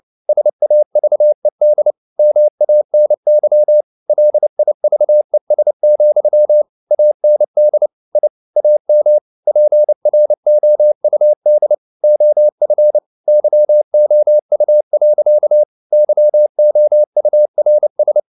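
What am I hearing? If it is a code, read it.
Morse code, heard as a keyed tone, about 29 wpm, 600 Hz. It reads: SAVED MANY LIVES, AND I AM PROUD OF YOU. YOURS
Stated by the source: Morse code at 29 wpm, sending Ken Follett's novel Never in English, from CW med Gnister.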